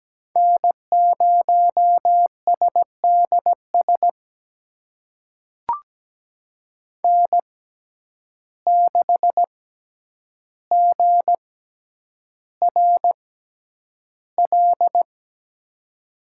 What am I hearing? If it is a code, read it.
Morse code, heard as a keyed tone, about 17 wpm, spaced as for 9 wpm, 700 Hz. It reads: N0SDS N6GRL
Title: USA Call Signs - 6x Character Spacing - 17wpm